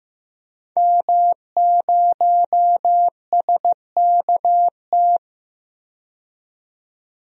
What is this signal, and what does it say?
Morse code, heard as a keyed tone, about 15 wpm, 700 Hz. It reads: M0SKT
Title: DXCC-100 Call Signs - 5 Characters - 15wpm